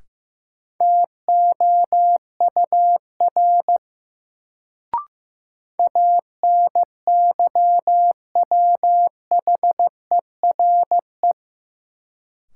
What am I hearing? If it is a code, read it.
Morse code, heard as a keyed tone, about 15 wpm, 700 Hz. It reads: TOUR ANYWHERE
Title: Sentences from Top 1400 Words - New Word Drill - 15wpm